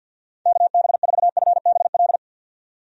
Morse code, heard as a keyed tone, about 53 wpm, 700 Hz. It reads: KB4FBL
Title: USA Call Signs - 6 Characters - 50wpm